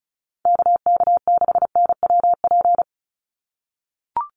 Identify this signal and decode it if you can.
Morse code, heard as a keyed tone, about 34 wpm, 700 Hz. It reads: KK6NWP
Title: USA Call Signs 35wpm